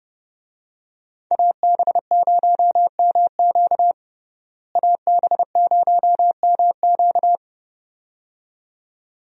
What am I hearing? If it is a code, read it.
Morse code, heard as a keyed tone, about 30 wpm, 700 Hz. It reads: AB0MQ AB0MQ
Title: USA Call Signs - 5 Characters - Repeated 2x - 30wpm